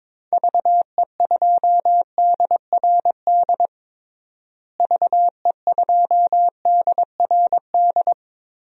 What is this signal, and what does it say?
Morse code, heard as a keyed tone, about 23 wpm, 700 Hz. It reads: VE2DRD VE2DRD
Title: DXCC-100 Call Signs - 6 Characters - Repeated 3x - 22wpm